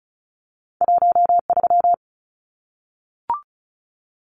Morse code, heard as a keyed tone, about 34 wpm, 700 Hz. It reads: 13
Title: Numbers - 2-Digit 35wpm